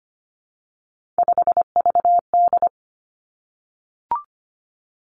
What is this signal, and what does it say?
Morse code, heard as a keyed tone, about 25 wpm, 700 Hz. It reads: 5VD
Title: Three Challenging Letter-Number Combinations 25wpm